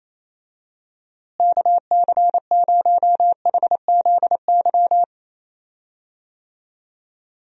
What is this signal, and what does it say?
Morse code, heard as a keyed tone, about 28 wpm, 700 Hz. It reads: KC0HZY
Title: USA Call Signs - 6 Characters - Repeated 2x - 28wpm